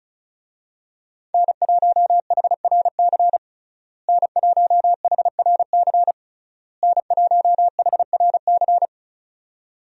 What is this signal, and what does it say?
Morse code, heard as a keyed tone, about 36 wpm, 700 Hz. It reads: N1HRC N1HRC N1HRC
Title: USA Call Signs - 5 Characters - Repeated 3x - 35wpm